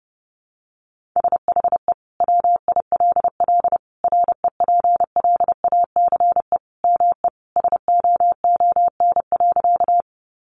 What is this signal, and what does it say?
Morse code, heard as a keyed tone, about 30 wpm, 700 Hz. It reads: SHE WILL REPLACE ME SOON.
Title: Sentences from Top 1400 Words 30wpm